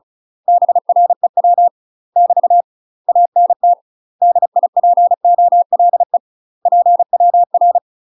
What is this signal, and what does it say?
Morse code, heard as a keyed tone, about 33 wpm, 700 Hz. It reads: BREW <BT> ANT DIPOLE PWR
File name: UK MASTER QSOs TRAINING / 35WPM / UK MASTER QSOs 0041